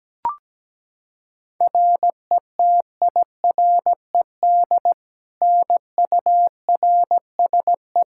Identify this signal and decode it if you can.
Morse code, heard as a keyed tone, about 17 wpm, 700 Hz. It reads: RETIRED NURSE
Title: Practice QSO - Exchange 3 - More Info - 17wpm